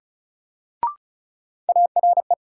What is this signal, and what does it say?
Morse code, heard as a keyed tone, about 34 wpm, 700 Hz. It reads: ARE
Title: Sentences from Top 100 Words - New Word Drill - 35wpm